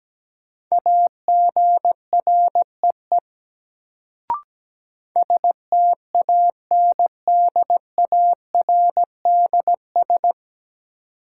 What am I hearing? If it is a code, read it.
Morse code, heard as a keyed tone, about 17 wpm, 700 Hz. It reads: AGREE STANDARDS
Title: Sentences from Top 1000 Words - Review Part 6 - New Word Drill - 17wpm